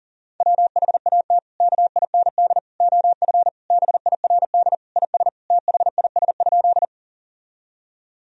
Morse code, heard as a keyed tone, about 40 wpm, 700 Hz. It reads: WHAT KIND OF BIRD IS THIS?
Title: Sentences from Top 1000 Words - Review Part 2 - 40wpm